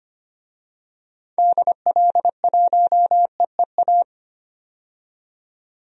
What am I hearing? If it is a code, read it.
Morse code, heard as a keyed tone, about 25 wpm, 700 Hz. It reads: DL1EEA
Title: DXCC-100 Call Signs - 6 Characters - 25wpm